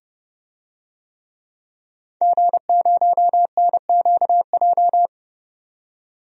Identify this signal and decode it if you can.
Morse code, heard as a keyed tone, about 30 wpm, 700 Hz. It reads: G0NQJ
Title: DXCC-100 Call Signs - 5 Characters - 30wpm